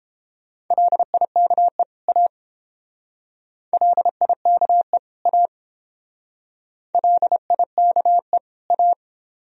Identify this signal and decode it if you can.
Morse code, heard as a keyed tone, about 29 wpm, 700 Hz. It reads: LIKE A LIKE A LIKE A